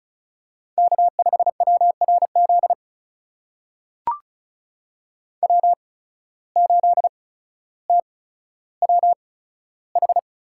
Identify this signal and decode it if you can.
Morse code, heard as a keyed tone, about 36 wpm, spaced as for 15 wpm, 700 Hz. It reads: K5WRZ W8TWH
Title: USA Call Signs - 5 Characters - 8x Character Spacing - 35wpm